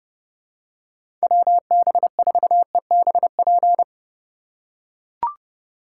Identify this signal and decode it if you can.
Morse code, heard as a keyed tone, about 31 wpm, 700 Hz. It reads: WB4EBP
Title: USA Call Signs - 6 Characters - 3x Character Spacing - 30wpm